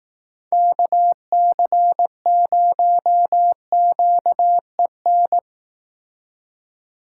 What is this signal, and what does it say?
Morse code, heard as a keyed tone, about 18 wpm, 700 Hz. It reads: KC0QEN
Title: Speed Racing - USA Call Signs 15wpm